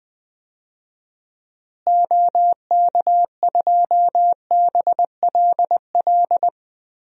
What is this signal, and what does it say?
Morse code, heard as a keyed tone, about 19 wpm, 700 Hz. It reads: OK2BLL
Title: DXCC-100 Call Signs - 6 Characters - Repeated 3x - 20wpm